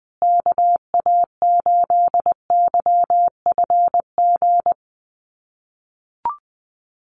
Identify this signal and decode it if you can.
Morse code, heard as a keyed tone, about 20 wpm, 700 Hz. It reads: KA8YFG
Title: USA Call Signs Repeated 2x 20wpm